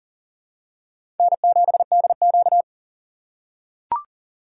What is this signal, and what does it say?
Morse code, heard as a keyed tone, about 37 wpm, 700 Hz. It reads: N7DQ
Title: USA Call Signs - 4 Characters - 6x Character Spacing - 40wpm